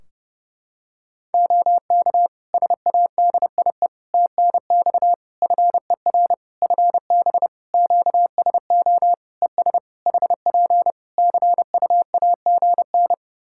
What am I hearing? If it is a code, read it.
Morse code, heard as a keyed tone, about 30 wpm, 700 Hz. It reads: OK SADIE TNX FER FB QSO ES HP CU AGN
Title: Practice QSO - Exchange 4 - Ending - 30wpm